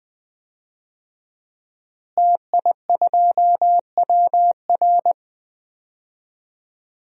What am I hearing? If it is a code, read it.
Morse code, heard as a keyed tone, about 20 wpm, 700 Hz. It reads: TI2WR
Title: DXCC-100 Call Signs - 5 Characters - Repeated 3x - 20wpm